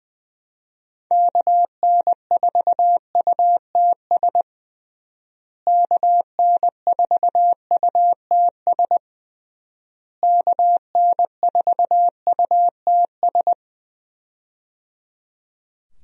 Morse code, heard as a keyed tone, about 20 wpm, 700 Hz. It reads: KN4UTS KN4UTS KN4UTS
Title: USA Call Signs - 6 Characters - Repeated 3x - 20wpm